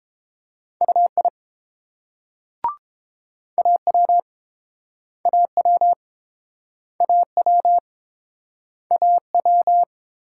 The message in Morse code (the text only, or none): UI AW AW AW AW